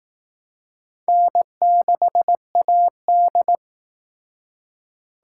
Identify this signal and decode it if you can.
Morse code, heard as a keyed tone, about 19 wpm, 700 Hz. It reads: N6AD
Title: Speed Racing - USA 1x2 Call Signs 15wpm